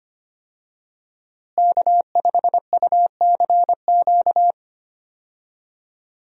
Morse code, heard as a keyed tone, about 25 wpm, 700 Hz. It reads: K5UCQ
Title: USA Call Signs - 5 Characters - Repeated 3x - 25wpm